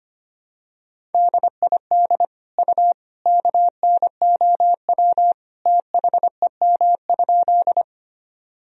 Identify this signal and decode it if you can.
Morse code, heard as a keyed tone, about 25 wpm, 700 Hz. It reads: DID U KNOW THEM?
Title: Sentences from Top 100 Words and Abbreviations 25wpm